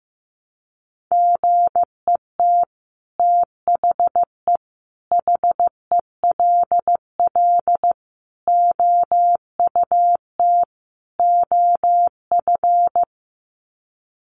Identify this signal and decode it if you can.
Morse code, heard as a keyed tone, about 15 wpm, 700 Hz. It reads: GET THE HELL OUT OF
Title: Sets of 5 Words Encore2 15wpm